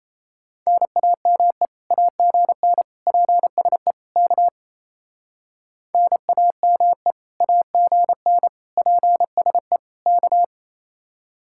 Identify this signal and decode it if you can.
Morse code, heard as a keyed tone, about 36 wpm, 700 Hz. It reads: NAME AGN PSE K NAME AGN PSE K